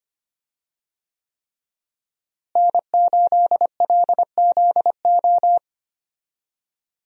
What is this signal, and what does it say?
Morse code, heard as a keyed tone, about 25 wpm, 700 Hz. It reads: N8LZO
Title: USA Call Signs - 5 Characters - Repeated 3x - 25wpm